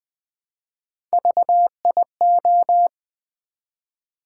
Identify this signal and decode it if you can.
Morse code, heard as a keyed tone, about 20 wpm, 700 Hz. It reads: VIO
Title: Build A Word - 3 Letters At A Time - 9-Letter Words - 20wpm